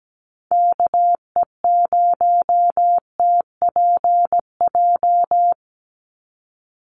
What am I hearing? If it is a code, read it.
Morse code, heard as a keyed tone, about 17 wpm, 700 Hz. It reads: KE0TPJ
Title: USA Call Signs 17wpm